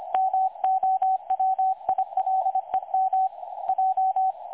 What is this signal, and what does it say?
Morse code, heard as a keyed tone, about 25 wpm, 760 Hz. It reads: MOWILEM J
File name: websdr recording 2018-03-25T20 27 14Z 3565.0kHz